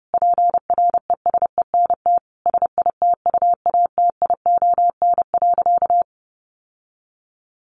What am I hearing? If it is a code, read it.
Morse code, heard as a keyed tone, about 28 wpm, 700 Hz. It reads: PRESENT SITUATION.
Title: Sentences from Top 1200 Words 30wpm